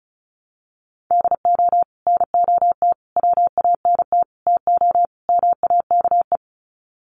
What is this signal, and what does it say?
Morse code, heard as a keyed tone, about 34 wpm, 700 Hz. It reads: DO NOT WANT TO MAKE
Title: Sets of 5 Words Encore 35wpm